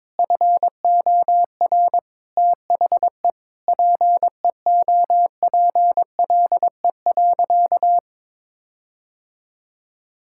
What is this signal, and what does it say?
Morse code, heard as a keyed tone, about 22 wpm, 700 Hz. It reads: FOR THE PEOPLE.
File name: Sentences with 9-Word Phrases - 22wpm